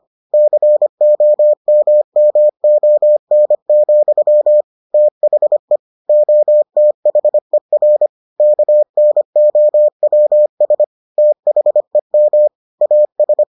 Morse code, heard as a keyed tone, about 25 wpm, 600 Hz. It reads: COMMON, THE OTHER KNOWS THEM AS